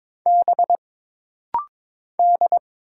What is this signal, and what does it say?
Morse code, heard as a keyed tone, about 22 wpm, 700 Hz. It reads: B D
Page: This rapid-fire timing is geared to push your ICR (Instant Character Recognition) skills.